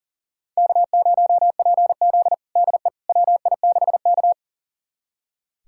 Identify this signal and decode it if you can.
Morse code, heard as a keyed tone, about 40 wpm, 700 Hz. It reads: K0PZ DE WI6K
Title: USA Callsign DE Callsign - 4 Characters - 40wpm